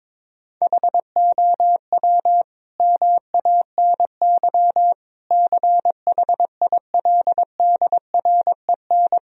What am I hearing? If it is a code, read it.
Morse code, heard as a keyed tone, about 22 wpm, 700 Hz. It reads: HOW MANY CHILDREN
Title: Sentences from Top 1000 Words - Review Part 1 - 22wpm